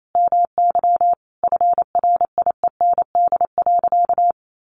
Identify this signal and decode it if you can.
Morse code, heard as a keyed tone, about 28 wpm, 700 Hz. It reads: MY FRIEND.
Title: Sentences from Top 1300 Words 28wpm